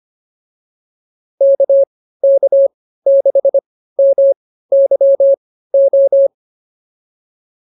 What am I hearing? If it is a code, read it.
Morse code, heard as a keyed tone, about 25 wpm, spaced as for 15 wpm, 550 Hz. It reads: KK6MYO
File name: POTA103 15